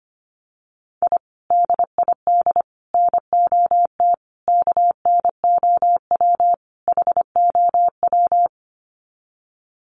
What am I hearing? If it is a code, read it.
Morse code, heard as a keyed tone, about 25 wpm, 700 Hz. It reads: I DID NOT KNOW HOW
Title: Sets of 5 Words Encore 25wpm